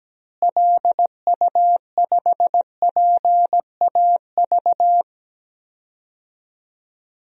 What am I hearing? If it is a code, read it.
Morse code, heard as a keyed tone, about 17 wpm, 700 Hz. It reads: LU5PAV
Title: DXCC-100 Call Signs - 6 Characters - Repeated 2x - 17wpm